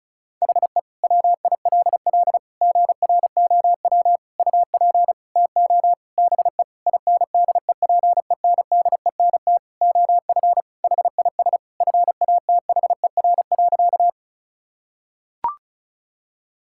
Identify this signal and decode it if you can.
Morse code, heard as a keyed tone, about 36 wpm, 700 Hz. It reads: HE WILL GROW UP TO BE INDEPENDENT OF HIS FATHER.
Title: Sentences from Top 1000 Words - Review Part 6 - 35wpm